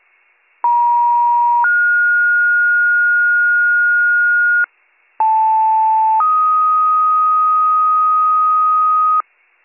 Two-tone page, heard as a one-second tone, then a three-second tone, about 950 then 1430 Hz, about 880 then 1250 Hz, with noise. It is heard from 0.6 to 9.2 s.